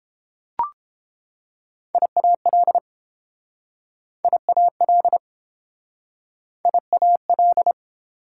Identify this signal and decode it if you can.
Morse code, heard as a keyed tone, about 29 wpm, 700 Hz. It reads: IAL IAL IAL